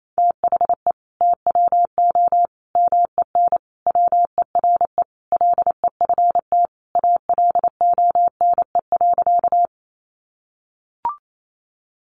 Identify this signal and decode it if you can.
Morse code, heard as a keyed tone, about 28 wpm, 700 Hz. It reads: THE TWO MEN WERE LEFT ALONE.
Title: Sentences from Top 1000 Words 28wpm